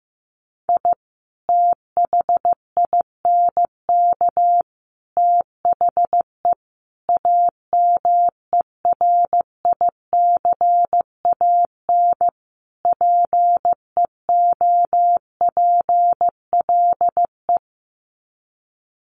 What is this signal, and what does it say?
Morse code, heard as a keyed tone, about 15 wpm, 700 Hz. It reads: I THINK THE AMERICAN PEOPLE
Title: Sets of 5 Words Encore 15wpm